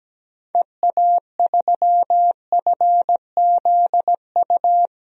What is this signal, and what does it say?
Morse code, heard as a keyed tone, about 17 wpm, 700 Hz. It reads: EA3FZU
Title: DXCC-100 Call Signs - 6 Characters - 17wpm